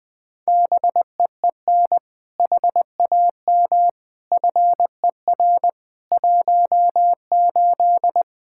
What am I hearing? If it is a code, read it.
Morse code, heard as a keyed tone, about 20 wpm, 700 Hz. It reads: BEEN HAM FER 18
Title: Practice QSO - Exchange 3 - More Info - 20wpm